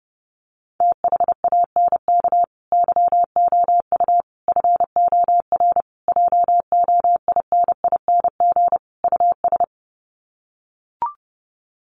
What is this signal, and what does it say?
Morse code, heard as a keyed tone, about 30 wpm, 700 Hz. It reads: THANK YOU FOR JOINING US
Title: Sets of 5 Words 30wpm